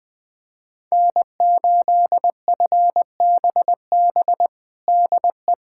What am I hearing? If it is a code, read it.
Morse code, heard as a keyed tone, about 20 wpm, 700 Hz. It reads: N8FBB DE
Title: USA Callsign DE Callsign - 5 Characters - 20wpm